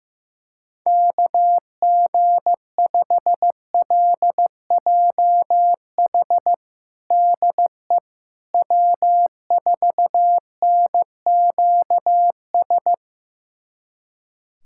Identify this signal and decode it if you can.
Morse code, heard as a keyed tone, about 15 wpm, 700 Hz. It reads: KG5LJH DE W4NQS
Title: USA Callsign DE Callsign 15wpm